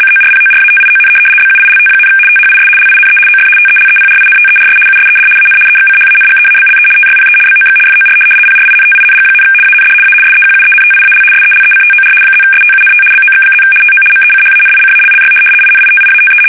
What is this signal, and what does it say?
RTTY, about 75 baud. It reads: CORPORATION BUILT, THIS LINE OF TELEPRINTERS USED AN EXCHANGEABLE TYPE BOX FOR PRINTING, AND A SEQUENTIAL SELECTOR +STUNT BOX+ TO MECHANICALLY INITIATE
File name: fsk850-75B / teletype-corp